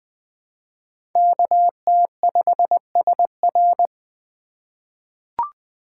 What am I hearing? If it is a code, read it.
Morse code, heard as a keyed tone, about 20 wpm, 700 Hz. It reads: KT5SR
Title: USA Call Signs - 5 Characters - 2x Character Spacing - 20wpm